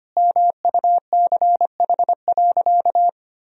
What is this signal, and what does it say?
Morse code, heard as a keyed tone, about 25 wpm, 700 Hz. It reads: MUCH.
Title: Sentences from Top 200 Words and Abbreviations 25wpm